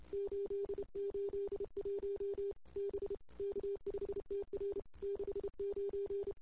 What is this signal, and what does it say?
Morse code, heard as a keyed tone, about 29 wpm, 400 Hz. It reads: 881 B K5TR 69
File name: CW / 2008-ss-cw-k5tr-op-n5rz -06:00